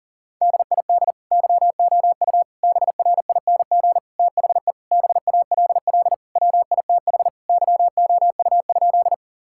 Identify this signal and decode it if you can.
Morse code, heard as a keyed tone, about 40 wpm, 700 Hz. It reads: DID YOU BRING THE BALL WITH YOU?
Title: Sentences from Top 1000 Words - Review Part 6 - 40wpm